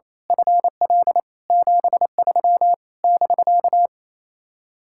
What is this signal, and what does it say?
Morse code, heard as a keyed tone, about 26 wpm, 700 Hz. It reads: FL 73 <BK>
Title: Speed-Racing Common QSO Phrases - 28wpm